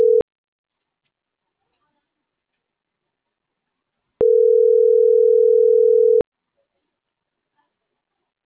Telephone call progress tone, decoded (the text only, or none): ringback tone